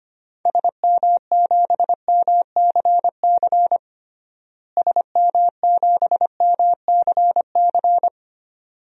Morse code, heard as a keyed tone, about 25 wpm, 700 Hz. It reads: SM7MCC SM7MCC
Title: DXCC-100 Call Signs - 6 Characters - Repeated 2x - 25wpm